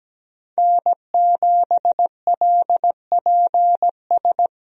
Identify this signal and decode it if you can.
Morse code, heard as a keyed tone, about 17 wpm, 700 Hz. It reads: N7LPS
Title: USA Call Signs - 5 Characters - Repeated 3x - 17wpm